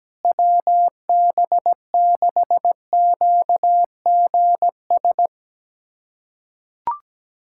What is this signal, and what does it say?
Morse code, heard as a keyed tone, about 17 wpm, 700 Hz. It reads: WB6QGS
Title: USA Callsign DE Callsign - 6 Characters - 17wpm